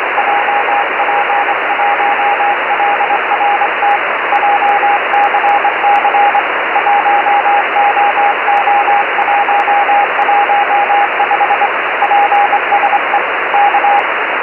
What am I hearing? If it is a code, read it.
Morse code, heard as a keyed tone, about 24 wpm, 810 Hz, with radio noise. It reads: JP1LRT JCC 1OO115 PSE K
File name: JP1LRT-430CW